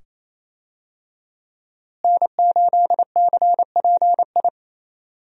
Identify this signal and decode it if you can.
Morse code, heard as a keyed tone, about 28 wpm, 700 Hz. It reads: N8CPI
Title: USA Call Signs - 5 Characters - Repeated 2x - 28wpm